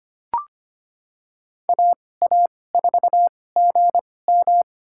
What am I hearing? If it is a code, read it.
Morse code, heard as a keyed tone, about 25 wpm, spaced as for 18 wpm, 700 Hz. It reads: AA4GM
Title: USA Call Signs - 5 Characters - 2x Character Spacing - 25wpm